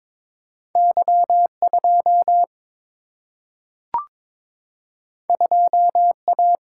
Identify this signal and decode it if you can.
Morse code, heard as a keyed tone, about 21 wpm, 700 Hz. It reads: Y 2 2 A
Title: Single-Letter Single-Number 22wpm